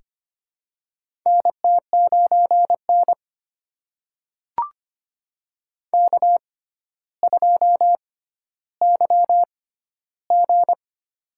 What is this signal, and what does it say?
Morse code, heard as a keyed tone, about 25 wpm, spaced as for 13 wpm, 700 Hz. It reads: NT9N K2YG
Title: USA Call Signs - 4 Characters - 6x Character Spacing - 25wpm